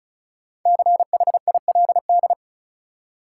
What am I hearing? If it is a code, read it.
Morse code, heard as a keyed tone, about 36 wpm, 700 Hz. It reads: CHILD
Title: Five Letter Words - 35wpm